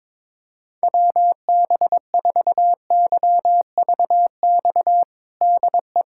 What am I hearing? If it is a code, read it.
Morse code, heard as a keyed tone, about 21 wpm, 700 Hz. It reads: WB4YVX DE
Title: USA Callsign DE Callsign - 6 Characters - 22wpm